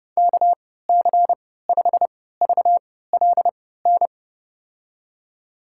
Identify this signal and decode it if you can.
Morse code, heard as a keyed tone, about 30 wpm, spaced as for 17 wpm, 700 Hz. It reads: KC5VLN
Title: USA Call Signs - 6 Characters - 3x Character Spacing - 30wpm